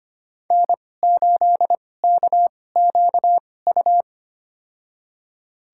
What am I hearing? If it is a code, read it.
Morse code, heard as a keyed tone, about 25 wpm, spaced as for 18 wpm, 700 Hz. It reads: N8KQU
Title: USA Call Signs - 5 Characters - 2x Character Spacing - 25wpm